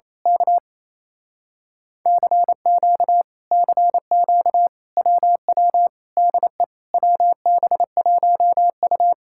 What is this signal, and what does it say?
Morse code, heard as a keyed tone, about 27 wpm, 700 Hz. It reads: K CQ CQ WW DE WB1U
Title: Speed-Racing Common QSO Phrases - 28wpm